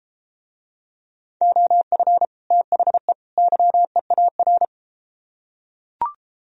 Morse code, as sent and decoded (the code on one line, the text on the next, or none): --- ..-. / - .... / -.-- . .- .-.
OF THE YEAR